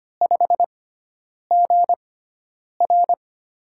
Morse code, heard as a keyed tone, about 25 wpm, spaced as for 9 wpm, 700 Hz. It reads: 5GR
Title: USA Call Signs - 4 Characters - 6x Character Spacing - 25wpm